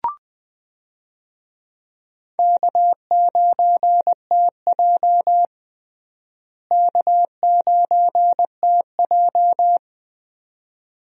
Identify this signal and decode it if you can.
Morse code, heard as a keyed tone, about 20 wpm, 700 Hz. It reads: K9TJ K9TJ
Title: USA Call Signs - 4 Characters - Repeated 2x - 20wpm